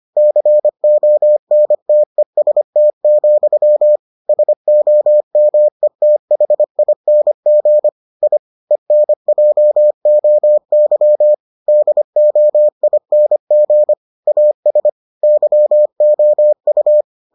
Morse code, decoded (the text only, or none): CONTEST, SOMETHING I ENJOY DOING AS YOU